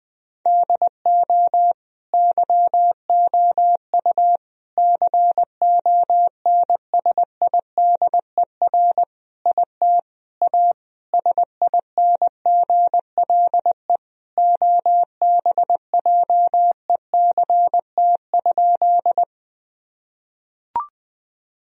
Morse code, heard as a keyed tone, about 20 wpm, 700 Hz. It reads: DO YOU CONSIDER IT A SINGLE OBJECT?
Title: Sentences from Top 1000 Words - Review Part 4 - 20wpm